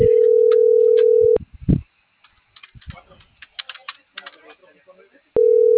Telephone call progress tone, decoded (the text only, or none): ringback tone